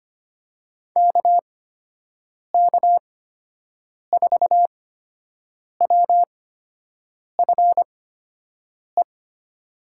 Morse code, heard as a keyed tone, about 25 wpm, spaced as for 7 wpm, 700 Hz. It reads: KK4WFE
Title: USA Call Signs - 8x Character Spacing - 25wpm